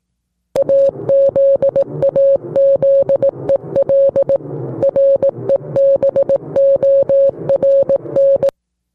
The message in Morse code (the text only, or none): AZAZEL REBORN